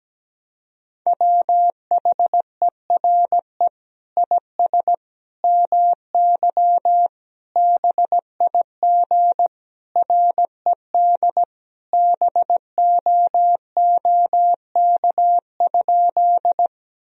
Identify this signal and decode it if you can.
Morse code, heard as a keyed tone, about 17 wpm, 700 Hz. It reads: WHERE IS MY BIG RED BOOK?
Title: Sentences from Top 1000 Words - Review Part 2 - 17wpm